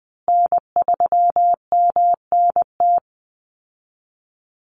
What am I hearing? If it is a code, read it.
Morse code, heard as a keyed tone, about 20 wpm, 700 Hz. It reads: N3MNT
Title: Call Signs Repeated 2x - 20wpm